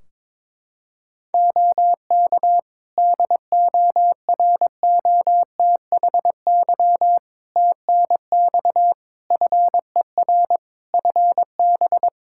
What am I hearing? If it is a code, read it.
Morse code, heard as a keyed tone, about 22 wpm, 700 Hz. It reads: OK DOROTHY TNX FER FB
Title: Practice QSO - Exchange 4 - Ending - 22wpm